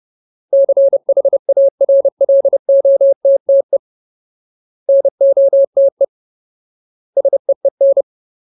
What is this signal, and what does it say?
Morse code, heard as a keyed tone, about 30 wpm, 550 Hz. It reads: CHARLOTTE NOTE SEEN
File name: Ultimate Word List / 30WPM Ultimate